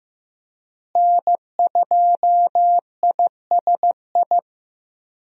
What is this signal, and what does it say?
Morse code, heard as a keyed tone, about 15 wpm, 700 Hz. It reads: N2ISI
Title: USA Call Signs - 5 Characters - Repeated 3x - 15wpm